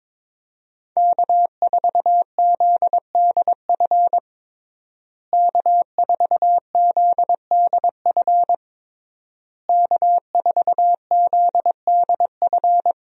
Morse code, heard as a keyed tone, about 22 wpm, 700 Hz. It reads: K4ZDF K4ZDF K4ZDF
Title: USA Call Signs - 5 Characters - Repeated 3x - 22wpm